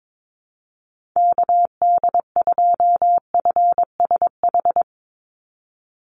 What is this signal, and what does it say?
Morse code, heard as a keyed tone, about 22 wpm, 700 Hz. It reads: KD2FSH
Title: Call Signs Repeated 2x - 22wpm